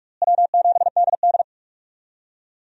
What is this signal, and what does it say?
Morse code, heard as a keyed tone, about 46 wpm, 700 Hz. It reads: W7DD
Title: USA Call Signs - 4 Characters - Repeated 3x - 45wpm